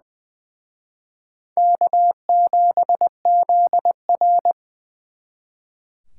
Morse code, heard as a keyed tone, about 24 wpm, 700 Hz. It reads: K7ZR
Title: Speed Racing - USA 1x2 Call Signs 20wpm